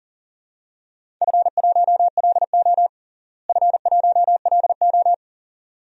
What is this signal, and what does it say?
Morse code, heard as a keyed tone, about 40 wpm, 700 Hz. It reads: F1LO F1LO
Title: DXCC-100 Call Signs - 4 Characters - Repeated 2x - 40wpm